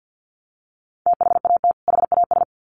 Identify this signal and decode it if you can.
Morse code, heard as a keyed tone, about 50 wpm, 700 Hz. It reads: THAT HAS
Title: Sets of 2 Words 50wpm